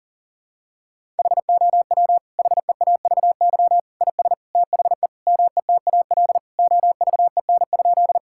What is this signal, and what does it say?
Morse code, heard as a keyed tone, about 40 wpm, 700 Hz. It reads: HOW HEAVY IS THE METAL OVEN?
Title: Sentences from Top 1000 Words - Review Part 6 - 40wpm